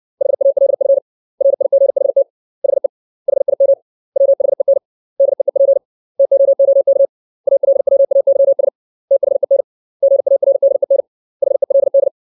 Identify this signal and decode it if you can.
Morse code, heard as a keyed tone, about 60 wpm, 550 Hz. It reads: HALF RIGHT HE HIM WHEN DEEP TOOK ALWAYS THEN GARDEN HELD
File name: 632 words / 632 Words 60WPM